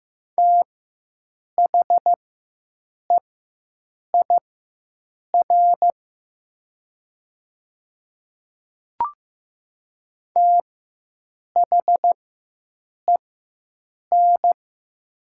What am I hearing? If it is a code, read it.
Morse code, heard as a keyed tone, about 15 wpm, spaced as for 7 wpm, 700 Hz. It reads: THEIR THEN
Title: 4x Character Spacing - Top 500 Words - 15wpm